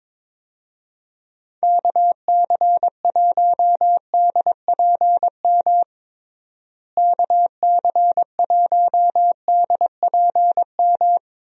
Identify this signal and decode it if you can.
Morse code, heard as a keyed tone, about 21 wpm, 700 Hz. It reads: KC1DPM KC1DPM
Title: USA Call Signs - 6 Characters - Repeated 3x - 22wpm